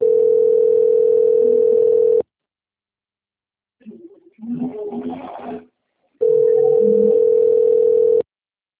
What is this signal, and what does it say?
Ringback tone (North America), 440 + 480 Hz.